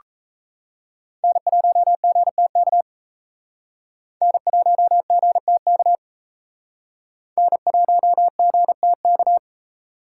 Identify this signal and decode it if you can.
Morse code, heard as a keyed tone, about 34 wpm, 700 Hz. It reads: N1GTK N1GTK N1GTK
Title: Speed Racing - USA 1x3 Call Signs 28wpm